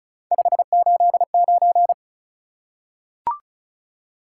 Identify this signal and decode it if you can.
Morse code, heard as a keyed tone, about 36 wpm, 700 Hz. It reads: 589
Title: Top 180 QSO Elements 35wpm